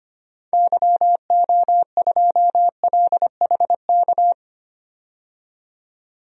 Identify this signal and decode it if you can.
Morse code, heard as a keyed tone, about 25 wpm, 700 Hz. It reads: YO2LHK
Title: DXCC-100 Call Signs - 6 Characters - 25wpm